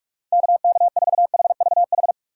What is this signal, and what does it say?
Morse code, heard as a keyed tone, about 46 wpm, 700 Hz. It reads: KK4HVH
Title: USA Call Signs - 6x Character Spacing - 45wpm